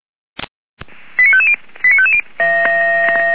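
Five-tone selective call (ZVEI), heard as quick runs of five tones, tones of about 70 ms. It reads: 89550 89550
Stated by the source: ZVEI five-tone selective call received by SDR.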